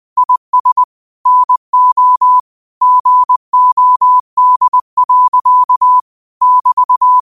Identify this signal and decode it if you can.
Morse code, heard as a keyed tone, about 20 wpm, 1000 Hz. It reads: IS NO GOD. <BT>
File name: qotdfiles / 20wpm